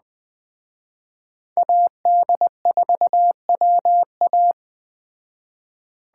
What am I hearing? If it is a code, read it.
Morse code, heard as a keyed tone, about 19 wpm, 700 Hz. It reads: AD4WA